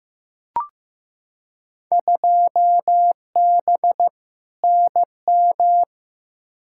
Morse code, heard as a keyed tone, about 15 wpm, 700 Hz. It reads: 2B NM
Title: Field Day Exchanges Encore - 15wpm